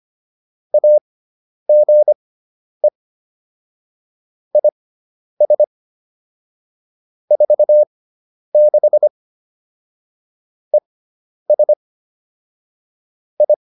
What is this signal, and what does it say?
Morse code, heard as a keyed tone, about 25 wpm, spaced as for 10 wpm, 600 Hz. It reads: AGE IS 46 ES I
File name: QSO302 10